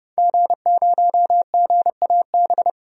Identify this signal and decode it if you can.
Morse code, heard as a keyed tone, about 32 wpm, 700 Hz. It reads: G0GAB
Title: DXCC-100 Call Signs - 5 Characters - 30wpm